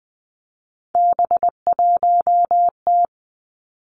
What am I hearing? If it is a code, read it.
Morse code, heard as a keyed tone, about 20 wpm, 700 Hz. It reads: B1T
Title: Three Challenging Letter-Number Combinations 20wpm